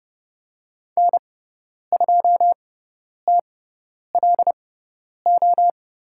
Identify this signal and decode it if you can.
Morse code, heard as a keyed tone, about 30 wpm, spaced as for 10 wpm, 700 Hz. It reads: N2TLO
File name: Lesson 18g - Letter m - Call Signs 30wpm